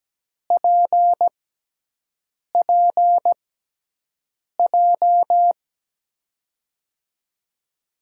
Morse code, heard as a keyed tone, about 17 wpm, spaced as for 6 wpm, 700 Hz. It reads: PPJ
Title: USA Call Signs - 6x Character Spacing - 17wpm